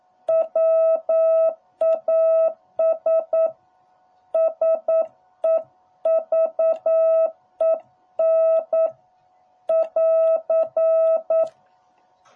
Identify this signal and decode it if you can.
Morse code, heard as a keyed tone, about 9 wpm, 660 Hz. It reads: WAS SEVEN <AR>